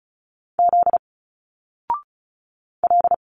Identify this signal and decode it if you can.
Morse code, heard as a keyed tone, about 34 wpm, 700 Hz. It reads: Z L
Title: Single Letters 35wpm